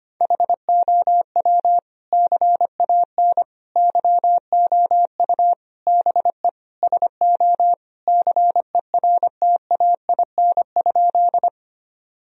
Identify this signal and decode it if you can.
Morse code, heard as a keyed tone, about 25 wpm, 700 Hz. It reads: HOW CAN YOU BE SO CERTAIN?
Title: Sentences from Top 1000 Words - Review Part 2 - 25wpm